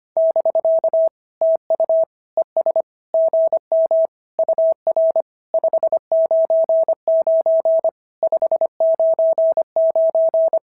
Morse code, heard as a keyed tone, about 25 wpm, 690 Hz. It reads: <BK> TU ES GM UR 599 599
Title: SOTA QSO Practice 25wpm